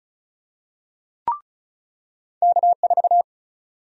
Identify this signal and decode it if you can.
Morse code, heard as a keyed tone, about 36 wpm, 700 Hz. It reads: K4
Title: USA Call Sign Prefixes - 35wpm